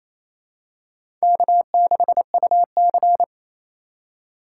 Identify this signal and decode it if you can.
Morse code, heard as a keyed tone, about 28 wpm, 700 Hz. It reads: K6UC
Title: USA Call Signs - 4 Characters - 4x Character Spacing - 28wpm